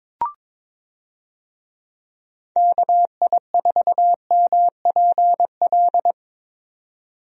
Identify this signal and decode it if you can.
Morse code, heard as a keyed tone, about 22 wpm, 700 Hz. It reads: KI4MPL
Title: USA Call Signs - 6 Characters - 22wpm